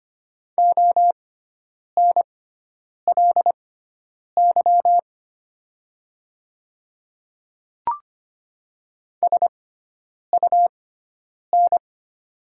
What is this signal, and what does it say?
Morse code, heard as a keyed tone, about 25 wpm, spaced as for 9 wpm, 700 Hz. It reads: ONLY SUN
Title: 6x Character Spacing - Top 500 Words - 25wpm